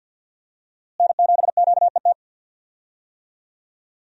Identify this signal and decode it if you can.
Morse code, heard as a keyed tone, about 53 wpm, 700 Hz. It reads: N7QET
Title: USA Call Signs - 5 Characters - Repeated 2x - 50wpm